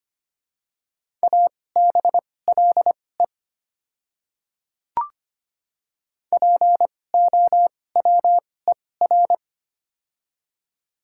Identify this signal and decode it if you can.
Morse code, heard as a keyed tone, about 25 wpm, spaced as for 18 wpm, 700 Hz. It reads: ABLE POWER
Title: 2x Character Spacing - Top 500 Words - 25wpm